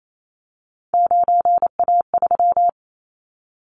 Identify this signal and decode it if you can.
Morse code, heard as a keyed tone, about 28 wpm, 700 Hz. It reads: 9A3
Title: Callsign Prefixes for Top 50 DXCC - 28wpm